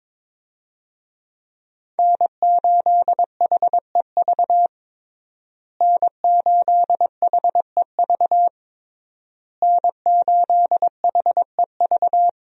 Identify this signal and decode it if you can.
Morse code, heard as a keyed tone, about 22 wpm, 700 Hz. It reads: N8HEV N8HEV N8HEV
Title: USA Call Signs - 5 Characters - Repeated 3x - 22wpm